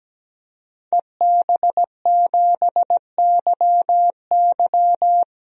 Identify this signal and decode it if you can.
Morse code, heard as a keyed tone, about 17 wpm, 700 Hz. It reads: EB7YY
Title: DXCC-100 Call Signs - 5 Characters - Repeated 3x - 17wpm